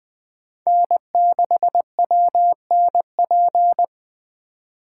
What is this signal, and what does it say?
Morse code, heard as a keyed tone, about 20 wpm, 700 Hz. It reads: N6WNP
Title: Speed Racing - USA 1x3 Call Signs 17wpm